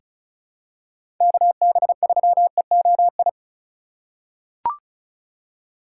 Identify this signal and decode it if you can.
Morse code, heard as a keyed tone, about 36 wpm, 700 Hz. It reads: KB3EOI
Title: USA Call Signs - 6 Characters - 4x Character Spacing - 35wpm